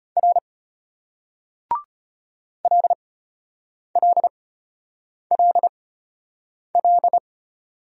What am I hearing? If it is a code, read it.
Morse code, heard as a keyed tone, about 34 wpm, spaced as for 8 wpm, 700 Hz. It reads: R LLLL